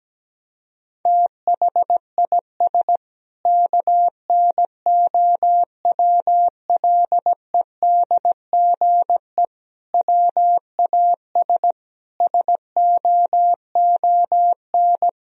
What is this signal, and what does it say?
Morse code, heard as a keyed tone, about 17 wpm, 700 Hz. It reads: THIS KNOWLEDGE WAS SOON
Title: Sentences from Top 1000 Words - Review Part 3 - 17wpm